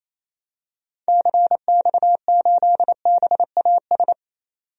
Morse code, heard as a keyed tone, about 28 wpm, 700 Hz. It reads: CX8BAS
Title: DXCC-100 Call Signs - 6 Characters - 28wpm